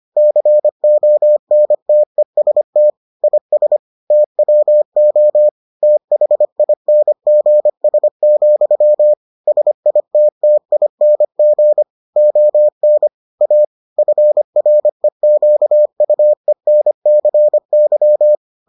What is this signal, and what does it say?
Morse code, heard as a keyed tone, about 25 wpm, 600 Hz. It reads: CONTEST IS TWO THINGS, SITTING ON A FREQUENCY